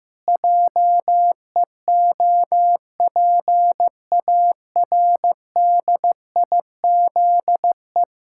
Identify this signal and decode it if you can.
Morse code, heard as a keyed tone, about 15 wpm, 700 Hz. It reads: JEOPARDIZE
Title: Pangram Challenge v2 - 15wpm